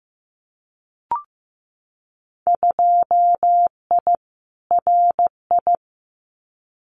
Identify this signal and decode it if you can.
Morse code, heard as a keyed tone, about 15 wpm, 700 Hz. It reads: 2I RI
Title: Winter Field Day Exchanges Encore 15wpm